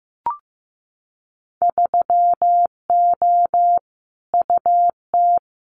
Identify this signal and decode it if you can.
Morse code, heard as a keyed tone, about 15 wpm, 700 Hz. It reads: 3O UT
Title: Winter Field Day Exchanges 15wpm